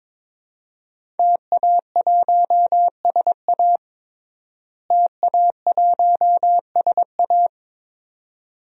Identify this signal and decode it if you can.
Morse code, heard as a keyed tone, about 22 wpm, 700 Hz. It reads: TA1SA TA1SA
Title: DXCC-100 Call Signs - 5 Characters - Repeated 2x - 22wpm